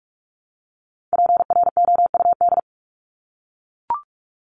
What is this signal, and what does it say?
Morse code, heard as a keyed tone, about 44 wpm, 700 Hz. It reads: PROUD
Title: Top 1501-1750 Words 45wpm